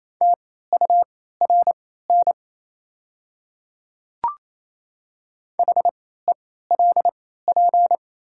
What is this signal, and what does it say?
Morse code, heard as a keyed tone, about 28 wpm, spaced as for 16 wpm, 700 Hz. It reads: TURN HELP
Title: 3x Character Spacing - Top 200 Words - 28wpm